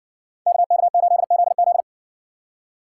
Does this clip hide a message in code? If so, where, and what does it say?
Yes: Morse code from 0.5 to 1.8 s, heard as a keyed tone, about 60 wpm, 700 Hz. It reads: KK8ZZ